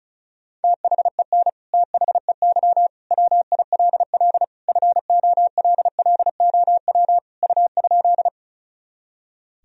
Morse code, heard as a keyed tone, about 36 wpm, 700 Hz. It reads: THEN THEY WILL FOLLOW U?
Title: Sentences from Top 200 Words and Abbreviations 35wpm